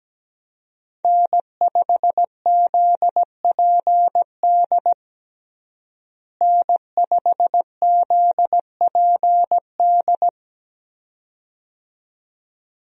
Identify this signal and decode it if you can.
Morse code, heard as a keyed tone, about 17 wpm, 700 Hz. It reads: N5ZPD N5ZPD
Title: USA Call Signs - 5 Characters - Repeated 2x - 17wpm